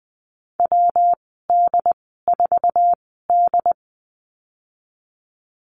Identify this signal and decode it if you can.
Morse code, heard as a keyed tone, about 20 wpm, spaced as for 15 wpm, 700 Hz. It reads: WD4D
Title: 128 Lesson 21g - Number 6 - Call Signs 20wpm